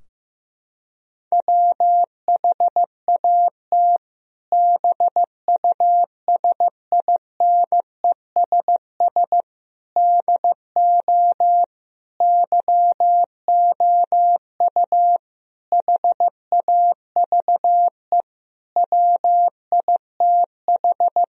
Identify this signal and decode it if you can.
Morse code, heard as a keyed tone, about 15 wpm, 700 Hz. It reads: WHAT BUSINESS DO YOU HAVE WITH
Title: Sentences from Top 1000 Words - Review Part 3 - 15wpm